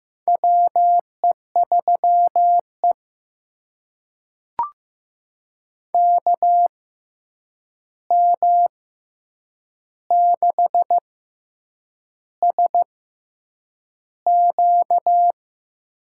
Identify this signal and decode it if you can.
Morse code, heard as a keyed tone, about 15 wpm, spaced as for 7 wpm, 700 Hz. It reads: WE3E KM6SQ
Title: USA Call Signs - 6x Character Spacing - 15wpm